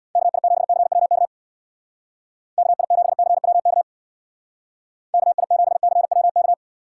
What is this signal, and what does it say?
Morse code, heard as a keyed tone, about 64 wpm, 700 Hz. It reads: KI7CWK KI7CWK KI7CWK